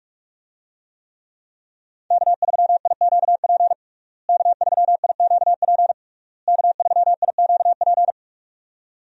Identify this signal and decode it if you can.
Morse code, heard as a keyed tone, about 46 wpm, 700 Hz. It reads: K3IQP K3IQP K3IQP
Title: USA Call Signs - 5 Characters - Repeated 3x - 45wpm